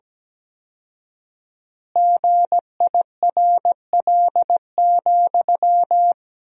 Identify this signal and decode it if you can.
Morse code, heard as a keyed tone, about 17 wpm, 700 Hz. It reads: GIRL,
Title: 4x Word-Spacing - Sentences from Top 400 Words - 17wpm